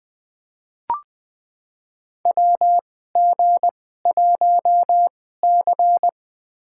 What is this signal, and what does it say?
Morse code, heard as a keyed tone, about 20 wpm, spaced as for 14 wpm, 700 Hz. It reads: WG1C
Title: USA Call Signs - 4 Characters - 2x Character Spacing - 20wpm